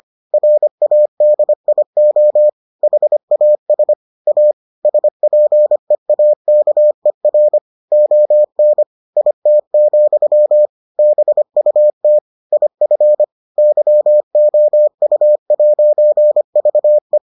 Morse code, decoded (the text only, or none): RADIO HAS A SPEAKER ON IT, BUT IF YOU'VE